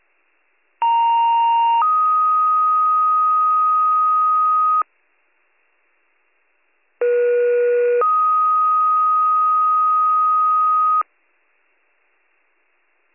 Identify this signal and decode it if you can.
Two-tone page, heard as a one-second tone, then a three-second tone, about 930 then 1250 Hz, about 480 then 1250 Hz. It is heard from 0.8 to 11.0 s.